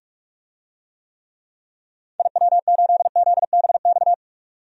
Morse code, heard as a keyed tone, about 46 wpm, 700 Hz. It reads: IW8ZBX
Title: DXCC-100 Call Signs - 6 Characters - 45wpm